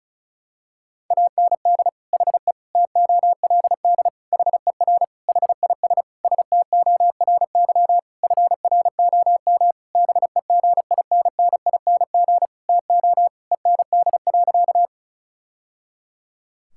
Morse code, heard as a keyed tone, about 34 wpm, 700 Hz. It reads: AND HE TOLD HER HIS STORY FROM BEGINNING TO END.
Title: Sentences from Top 1000 Words - Review Part 5 - 35wpm